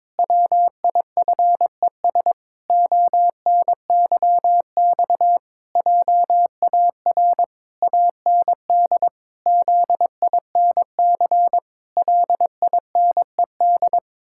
Morse code, heard as a keyed tone, about 22 wpm, 700 Hz. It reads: WIFES ONYX JAR AND ZINC LINED
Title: Pangram Challenge v2 - 22wpm